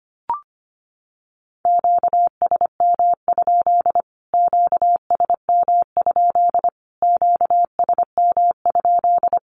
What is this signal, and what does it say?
Morse code, heard as a keyed tone, about 25 wpm, 700 Hz. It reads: QSM? QSM? QSM?
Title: Q-Codes Repeated 3x 25wpm